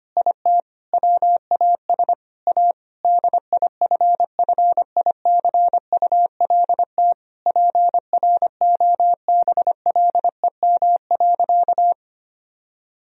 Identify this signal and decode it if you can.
Morse code, heard as a keyed tone, about 25 wpm, 700 Hz. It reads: IT WAS A DIFFICULT PROBLEM.
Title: Sentences from Top 1000 Words - Review Part 2 - 25wpm